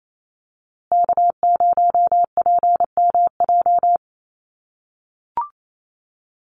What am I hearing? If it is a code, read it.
Morse code, heard as a keyed tone, about 28 wpm, 700 Hz. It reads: K0PMJ
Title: USA Call Signs Repeated 3x 28wpm